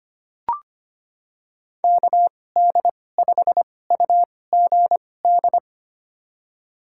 Morse code, heard as a keyed tone, about 25 wpm, spaced as for 18 wpm, 700 Hz. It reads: KD5UGD
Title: USA Call Signs - 6 Characters - 2x Character Spacing - 25wpm